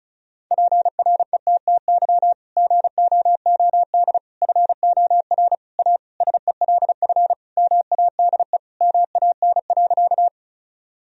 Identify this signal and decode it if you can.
Morse code, heard as a keyed tone, about 34 wpm, 700 Hz. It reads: PRETTY GOOD FOR A SELF MADE MAN.
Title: Sentences from Top 1000 Words - Review Part 1 - 35wpm